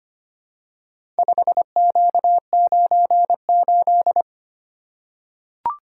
Morse code, heard as a keyed tone, about 25 wpm, 700 Hz. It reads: 5Q98
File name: practice / 25wpm / Four Letter-Number Combinations - 25wpm